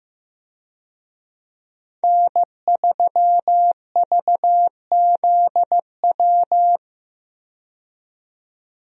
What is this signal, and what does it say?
Morse code, heard as a keyed tone, about 15 wpm, 700 Hz. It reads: N3VZW
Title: USA Call Signs - 5 Characters - 15wpm